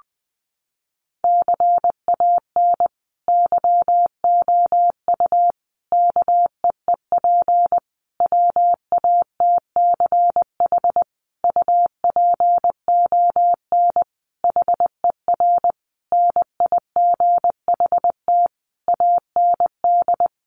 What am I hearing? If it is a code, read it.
Morse code, heard as a keyed tone, about 20 wpm, 700 Hz. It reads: CAN YOU KEEP WATCH UPON HER NIGHT AND
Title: Sentences from Top 1100 Words 20wpm